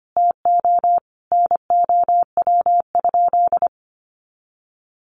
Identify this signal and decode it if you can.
Morse code, heard as a keyed tone, about 25 wpm, 700 Hz. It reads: TO NOW?
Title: Sentences from Top 100 Words 25wpm